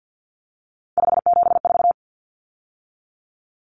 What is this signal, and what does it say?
Morse code, heard as a keyed tone, about 50 wpm, 700 Hz. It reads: 574
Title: Numbers - 3-Digit 50wpm